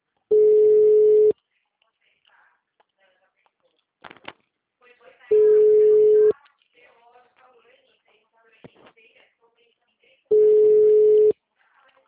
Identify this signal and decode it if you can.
Ringback tone (Europe, 425 Hz), with speech.
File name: out- -2010- - - .688